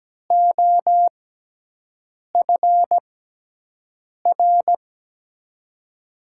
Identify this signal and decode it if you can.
Morse code, heard as a keyed tone, about 17 wpm, spaced as for 6 wpm, 700 Hz. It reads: OFR
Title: USA Call Signs - 6 Characters - 6x Character Spacing - 17wpm